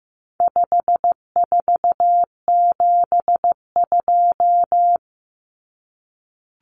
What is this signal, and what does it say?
Morse code, heard as a keyed tone, about 15 wpm, 700 Hz. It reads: 5472
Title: Common Cut Numbers - 4-Digit 15wpm